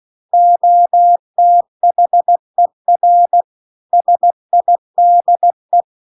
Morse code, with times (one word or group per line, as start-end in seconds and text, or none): 0.3-3.4 OTHER
3.9-5.8 SIDE